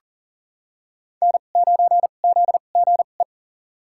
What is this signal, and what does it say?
Morse code, heard as a keyed tone, about 40 wpm, spaced as for 29 wpm, 700 Hz. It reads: N9ZGE
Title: USA Call Signs - 5 Characters - 2x Character Spacing - 40wpm